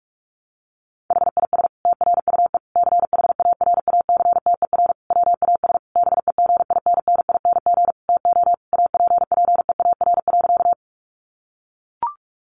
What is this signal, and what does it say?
Morse code, heard as a keyed tone, about 44 wpm, 700 Hz. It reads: HIS TRUE CHARACTER WAS BEGINNING TO APPEAR.
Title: Sentences from Top 1200 Words 45wpm